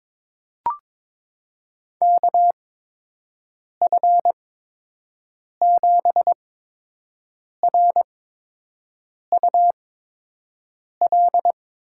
Morse code, heard as a keyed tone, about 23 wpm, spaced as for 6 wpm, 700 Hz. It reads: KF7RUL
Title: USA Call Signs - 6 Characters - 8x Character Spacing - 22wpm